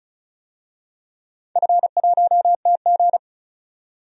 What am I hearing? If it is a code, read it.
Morse code, heard as a keyed tone, about 36 wpm, 700 Hz. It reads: F1TG
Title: DXCC-100 Call Signs - 4 Characters - 35wpm